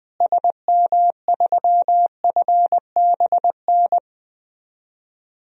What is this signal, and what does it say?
Morse code, heard as a keyed tone, about 21 wpm, 700 Hz. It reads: SM3FBN
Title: DXCC-100 Call Signs - 6 Characters - Repeated 2x - 20wpm